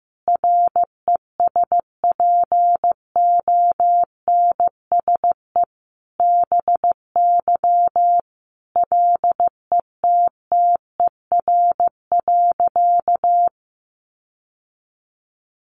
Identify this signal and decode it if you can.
Morse code, heard as a keyed tone, about 15 wpm, 700 Hz. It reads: RESPONSE BY LETTER.
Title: Sentences from Top 1400 Words 15wpm